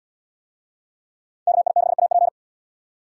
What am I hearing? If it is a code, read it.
Morse code, heard as a keyed tone, about 70 wpm, 700 Hz. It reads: KI6AP